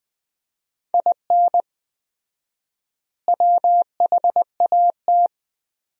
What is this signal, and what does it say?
Morse code, heard as a keyed tone, about 20 wpm, 700 Hz. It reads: IN WHAT
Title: 4x Word-Spacing - Sentences from Top 200 Words - 20wpm